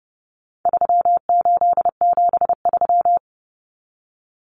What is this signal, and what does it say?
Morse code, heard as a keyed tone, about 30 wpm, 700 Hz. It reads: 3873
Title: Numbers - 4-Digit 30wpm